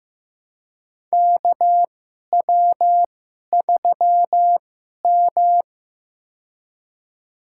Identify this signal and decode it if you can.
Morse code, heard as a keyed tone, about 15 wpm, spaced as for 11 wpm, 700 Hz. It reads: KW3M
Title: USA Call Signs - 2x Character Spacing - 15wpm